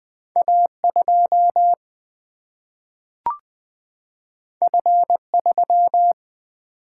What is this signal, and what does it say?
Morse code, heard as a keyed tone, about 20 wpm, 700 Hz. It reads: A2 F3
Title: Single-Letter Single-Number 20wpm